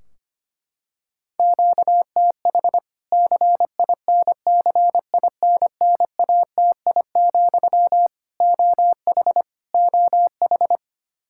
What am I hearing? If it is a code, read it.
Morse code, heard as a keyed tone, about 25 wpm, 700 Hz. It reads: QTH CINCINNATI, OH OH
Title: K1USN SST QSO Practice 25wpm